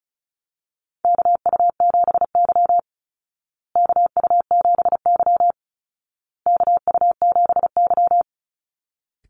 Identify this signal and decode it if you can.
Morse code, heard as a keyed tone, about 34 wpm, 700 Hz. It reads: KU7Y KU7Y KU7Y
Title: Call Signs Repeated 2x - 35wpm